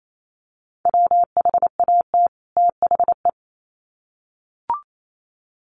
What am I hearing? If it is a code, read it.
Morse code, heard as a keyed tone, about 28 wpm, 700 Hz. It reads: WHAT THE
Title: Sets of 2 Words 28wpm